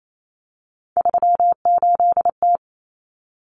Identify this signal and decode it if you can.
Morse code, heard as a keyed tone, about 28 wpm, 700 Hz. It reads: 38T
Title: Common Cut Numbers - 3-Digit 28wpm